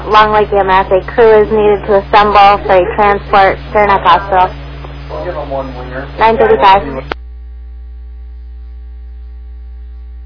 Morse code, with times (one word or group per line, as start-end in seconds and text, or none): none